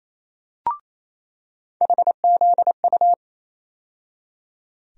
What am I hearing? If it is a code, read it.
Morse code, heard as a keyed tone, about 28 wpm, 700 Hz. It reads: HZU
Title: USA Call Sign Postfixes - 3 Letters 28wpm